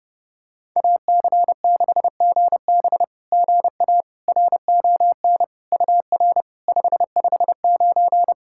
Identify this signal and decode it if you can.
Morse code, heard as a keyed tone, about 29 wpm, 690 Hz. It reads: AC6GB GA RON UR 559
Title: SOTA QSO Practice 30wpm